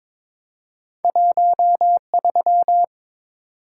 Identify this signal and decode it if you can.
Morse code, heard as a keyed tone, about 22 wpm, 700 Hz. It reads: 13
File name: Two Letter-Number Combinations - 22wpm